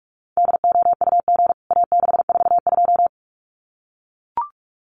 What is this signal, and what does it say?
Morse code, heard as a keyed tone, about 44 wpm, 700 Hz. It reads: DOUG A642
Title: CWT Exchange 45wpm